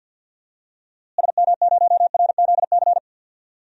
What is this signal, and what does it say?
Morse code, heard as a keyed tone, about 53 wpm, 700 Hz. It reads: SM0RZC
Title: DXCC-100 Call Signs - 6 Characters - Repeated 2x - 50wpm